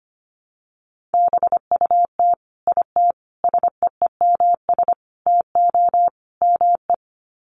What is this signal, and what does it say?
Morse code, heard as a keyed tone, about 25 wpm, 700 Hz. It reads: BUT IT SEEMS TO ME
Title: Sets of 5 Words Encore2 25wpm